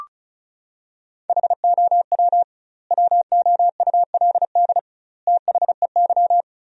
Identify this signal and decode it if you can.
Morse code, heard as a keyed tone, about 36 wpm, 700 Hz. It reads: HOW WOULD THEY
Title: 2x Word-Spacing - Sentences from Top 300 Words - 35wpm